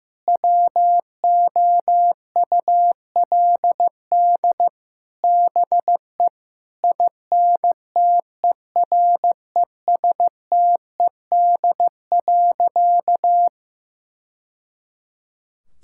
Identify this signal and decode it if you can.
Morse code, heard as a keyed tone, about 15 wpm, 700 Hz. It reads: WOULD BE INTERESTED.
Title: Sentences from Top 1000 Words - Review Part 6 - 15wpm